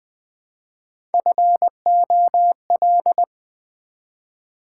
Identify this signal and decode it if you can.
Morse code, heard as a keyed tone, about 20 wpm, 700 Hz. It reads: FOL
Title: Build A Word - 1 Letter At A Time - 4-Letter Words - 20wpm